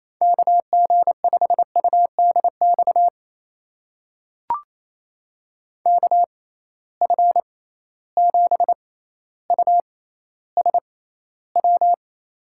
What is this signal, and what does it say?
Morse code, heard as a keyed tone, about 28 wpm, spaced as for 14 wpm, 700 Hz. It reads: KG5UDX KF7USW
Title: USA Call Signs - 6 Characters - 6x Character Spacing - 28wpm